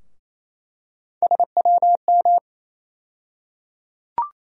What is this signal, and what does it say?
Morse code, heard as a keyed tone, about 28 wpm, 700 Hz. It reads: SWM
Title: USA Call Sign Postfixes - 3 Letters 28wpm